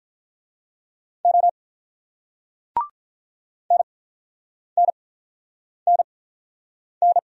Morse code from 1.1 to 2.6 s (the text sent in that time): O